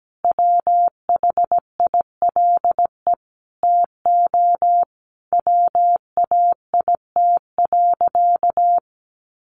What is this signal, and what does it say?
Morse code, heard as a keyed tone, about 17 wpm, 700 Hz. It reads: WHILE TO WAIT.
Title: Sentences from Top 800 Words 17wpm